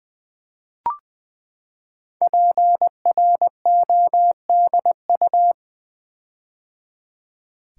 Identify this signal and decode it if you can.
Morse code, heard as a keyed tone, about 20 wpm, 700 Hz. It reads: PRODU
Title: Build A Word - 5 Letters At A Time - 12-Letter Words - 20wpm